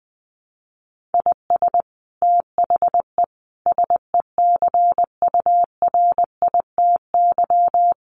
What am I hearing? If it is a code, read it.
Morse code, heard as a keyed tone, about 20 wpm, 700 Hz. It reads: IS THE SECURITY
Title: Sentences from Top 1400 Words 20wpm